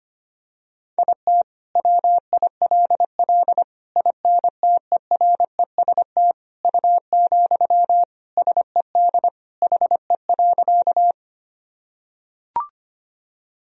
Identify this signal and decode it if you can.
Morse code, heard as a keyed tone, about 25 wpm, 700 Hz. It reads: IT WILL INTEREST U, SED HE.
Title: Sentences from Top 400 Words and Abbreviations 25wpm